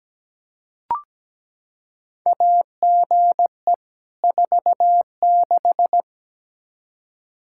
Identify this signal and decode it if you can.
Morse code, heard as a keyed tone, about 17 wpm, 700 Hz. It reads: AGE 46
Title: Practice QSO - Exchange 3 - More Info - 17wpm